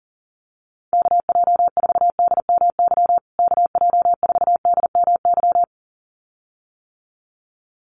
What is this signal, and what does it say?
Morse code, heard as a keyed tone, about 40 wpm, 700 Hz. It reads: KJ4DMY KJ4DMY
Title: USA Call Signs Repeated 2x 40wpm